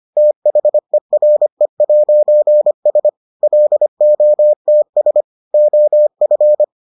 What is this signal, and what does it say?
Morse code, heard as a keyed tone, about 25 wpm, 600 Hz. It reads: THERE'S LOTS OF